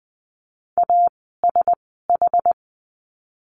Morse code, heard as a keyed tone, about 20 wpm, spaced as for 14 wpm, 700 Hz. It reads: ASH
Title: 061 Lesson 11f - Letter h - Words 20wpm